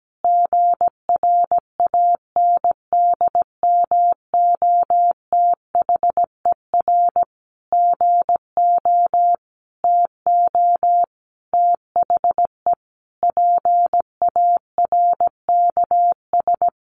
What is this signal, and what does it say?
Morse code, heard as a keyed tone, about 17 wpm, 700 Hz. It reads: GRANDMOTHER GO TO THE PARKS
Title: Additional Easy Phrases 17wpm